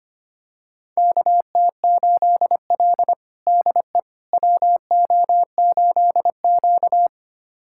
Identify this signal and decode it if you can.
Morse code, heard as a keyed tone, about 25 wpm, 700 Hz. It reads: KT8L DE WO8Q
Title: USA Callsign DE Callsign - 4 Characters - 25wpm